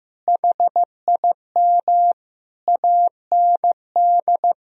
Morse code, heard as a keyed tone, about 15 wpm, 700 Hz. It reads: HIM AND
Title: Sentences from Top 1000 Words - Review Part 1 - 15wpm